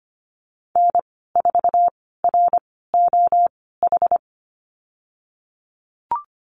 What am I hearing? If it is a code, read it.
Morse code, heard as a keyed tone, about 25 wpm, spaced as for 16 wpm, 700 Hz. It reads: N4ROH
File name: Lesson 15g - Number 5 - Call Signs 25wpm